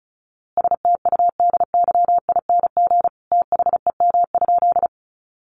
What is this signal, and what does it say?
Morse code, heard as a keyed tone, about 34 wpm, 700 Hz. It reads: STUDYING THEM?
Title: A Sound Of Thunder 35wpm